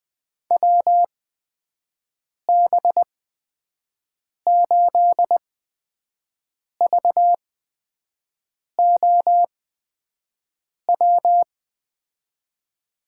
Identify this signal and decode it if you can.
Morse code, heard as a keyed tone, about 21 wpm, spaced as for 5 wpm, 700 Hz. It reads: WB8VOW
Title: USA Call Signs - 6 Characters - 8x Character Spacing - 20wpm